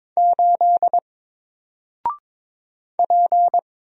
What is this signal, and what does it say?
Morse code, heard as a keyed tone, about 22 wpm, 700 Hz. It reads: 8 P
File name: Single-Letter-Number-22wpm